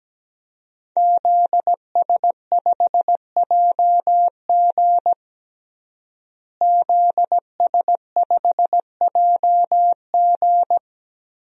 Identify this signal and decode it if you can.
Morse code, heard as a keyed tone, about 17 wpm, 700 Hz. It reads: ZS5JG ZS5JG
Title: DXCC-100 Call Signs - 5 Characters - Repeated 3x - 17wpm